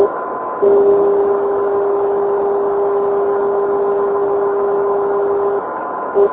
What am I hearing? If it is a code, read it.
Dial tone (Japan), 400 Hz, with noise.